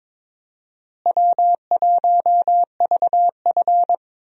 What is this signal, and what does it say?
Morse code, heard as a keyed tone, about 22 wpm, 700 Hz. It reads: W1VF
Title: USA Call Signs - 4 Characters - Repeated 3x - 22wpm